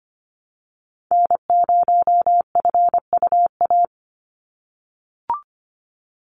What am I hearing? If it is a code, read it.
Morse code, heard as a keyed tone, about 25 wpm, 690 Hz. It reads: N0FUA
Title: USA Call Signs 25wpm